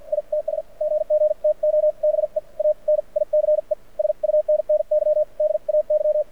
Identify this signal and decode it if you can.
Morse code, heard as a keyed tone, about 53 wpm, spaced as for 43 wpm, 600 Hz. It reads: ITS GG TO BE A NICE SUNNY DAY
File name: archived / CRSnet